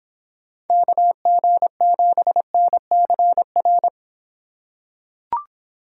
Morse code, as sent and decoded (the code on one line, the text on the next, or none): -.- --. --... -. -.-. .-.
KG7NCR